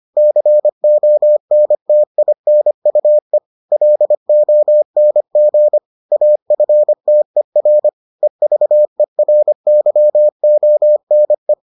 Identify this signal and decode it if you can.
Morse code, heard as a keyed tone, about 25 wpm, 600 Hz. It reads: CONTINUE LONG AFTER EVERYONE